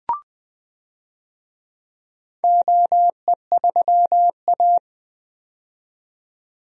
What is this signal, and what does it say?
Morse code, heard as a keyed tone, about 20 wpm, 700 Hz. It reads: OE3A
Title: DXCC-100 Call Signs - 4 Characters - 20wpm